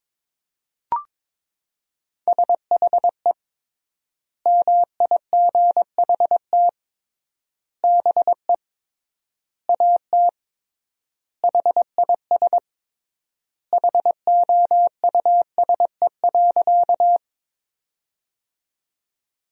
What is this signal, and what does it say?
Morse code, heard as a keyed tone, about 21 wpm, 700 Hz. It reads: SHE MIGHT BE AT HIS HOUSE.
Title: 3x Word-Spacing - Sentences from Top 300 Words - 22wpm